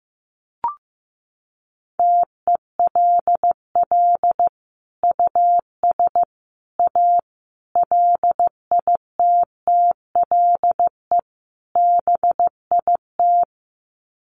Morse code, heard as a keyed tone, about 15 wpm, 700 Hz. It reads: TELL US A LITTLE BIT
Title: Sets of 5 Words Encore2 15wpm